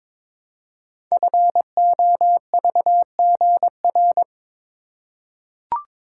Morse code, heard as a keyed tone, about 22 wpm, 690 Hz. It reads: FOVGR
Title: Code Groups - 5 Letters - 22wpm